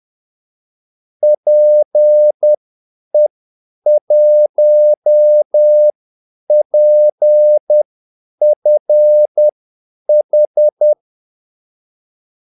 Morse code, heard as a keyed tone, about 10 wpm, spaced as for 8 wpm, 600 Hz. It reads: PE1PFH
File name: Callsigns-10-8wpm